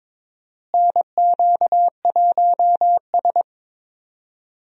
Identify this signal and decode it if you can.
Morse code, heard as a keyed tone, about 22 wpm, 700 Hz. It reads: NQ1S
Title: USA Call Signs - 4 Characters - Repeated 2x - 22wpm